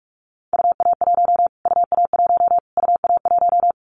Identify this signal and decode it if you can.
Morse code, heard as a keyed tone, about 44 wpm, 700 Hz. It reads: UA1 UA1 UA1
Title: Callsign Prefixes for Top 25 DXCC - Repeated 3x - 45wpm